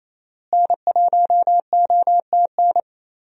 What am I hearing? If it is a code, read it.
Morse code, heard as a keyed tone, about 28 wpm, 700 Hz. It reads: N1OTN
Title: USA Call Signs - 5 Characters - Repeated 2x - 28wpm